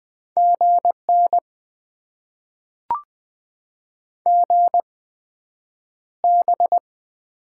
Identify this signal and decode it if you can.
Morse code, heard as a keyed tone, about 20 wpm, spaced as for 9 wpm, 700 Hz. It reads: GN GB
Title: Top 90 QSO Elements - 8x Character Spacing - 20wpm